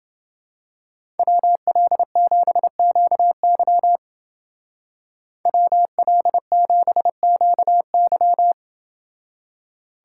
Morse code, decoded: WL7QY WL7QY

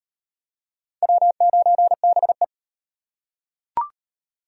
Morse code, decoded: W9BE

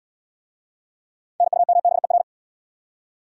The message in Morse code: NUMBER

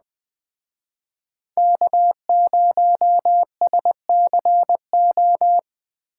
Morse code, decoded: K0SCO

